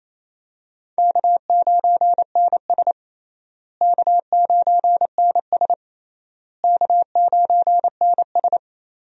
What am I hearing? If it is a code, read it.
Morse code, heard as a keyed tone, about 28 wpm, 700 Hz. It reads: K9NS K9NS K9NS
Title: USA Call Signs - 4 Characters - Repeated 3x - 28wpm